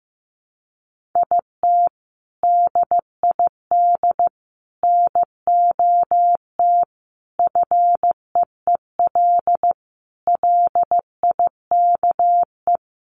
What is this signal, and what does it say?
Morse code, heard as a keyed tone, about 15 wpm, 700 Hz. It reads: IT DID NOT FEEL LIKE
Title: Sentences from Top 1100 Words 15wpm